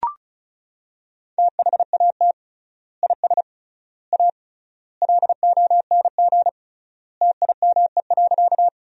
Morse code, heard as a keyed tone, about 36 wpm, 700 Hz. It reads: THAT IS A LONG TIME.
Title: 3x Word-Spacing - Sentences from Top 100 Words - 35wpm